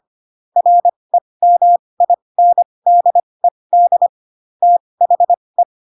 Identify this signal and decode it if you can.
Morse code, heard as a keyed tone, about 25 wpm, spaced as for 20 wpm, 700 Hz. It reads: REMINDED THE
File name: Pinocchio / Pinocchio ch5 15 25 ik1hge